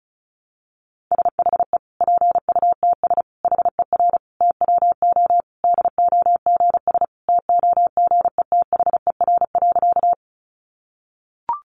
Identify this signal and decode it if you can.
Morse code, heard as a keyed tone, about 34 wpm, 700 Hz. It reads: SHE PUTS HER TWO DOGS TOGETHER.
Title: Easy Phrases 35wpm